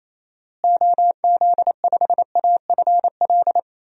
Morse code, heard as a keyed tone, about 28 wpm, 700 Hz. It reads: OZ5AFL
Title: DXCC-100 Call Signs - 6 Characters - Repeated 3x - 28wpm